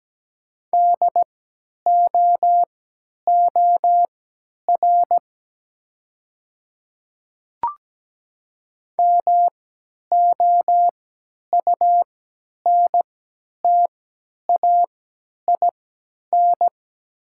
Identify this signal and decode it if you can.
Morse code, heard as a keyed tone, about 17 wpm, spaced as for 10 wpm, 700 Hz. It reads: DOOR MOUNTAIN
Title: 3x Character Spacing - Top 300 Words - 17wpm